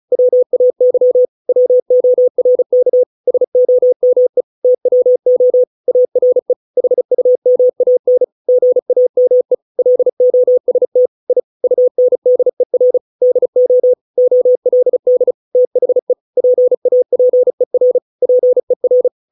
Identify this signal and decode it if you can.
Morse code, heard as a keyed tone, about 34 wpm, 500 Hz. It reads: WAY WORK SOME TWO ARE HUMAN GAME LOST I UNDER DO OLD THE PAPER PER
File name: 632 words / 632 Words 35WPM